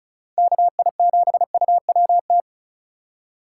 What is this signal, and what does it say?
Morse code, heard as a keyed tone, about 44 wpm, 700 Hz. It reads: KI7UWT